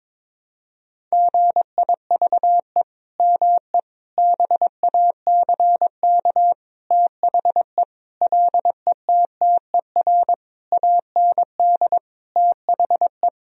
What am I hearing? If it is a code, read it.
Morse code, heard as a keyed tone, about 22 wpm, 700 Hz. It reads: GIVE ME BACK THE LETTER AND THE
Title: Sentences from Top 1000 Words - Review Part 4 - 22wpm